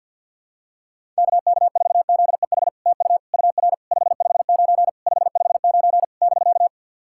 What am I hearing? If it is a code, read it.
Morse code, heard as a keyed tone, about 53 wpm, 700 Hz. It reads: KK4ZEH TU UR 559 559 <BK>
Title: SOTA QSO Practice 50wpm